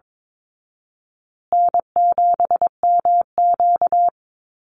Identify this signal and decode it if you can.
Morse code, heard as a keyed tone, about 22 wpm, 700 Hz. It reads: N7MQ
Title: Easy Call Signs Repeated 2x - 22wpm